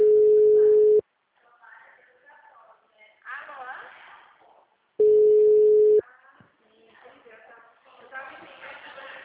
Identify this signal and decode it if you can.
Ringback tone (Europe, 425 Hz), with speech.
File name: out- -2045- - - .16111